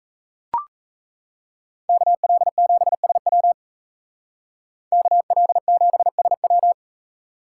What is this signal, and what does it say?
Morse code, heard as a keyed tone, about 33 wpm, 700 Hz. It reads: KL7SW KL7SW
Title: Speed Racing - USA 2x2 Call Signs 28wpm